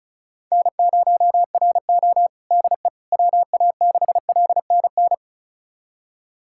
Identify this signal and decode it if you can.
Morse code, heard as a keyed tone, about 36 wpm, 700 Hz. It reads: N0RO DE WA6LNN
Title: USA Callsign DE Callsign 35wpm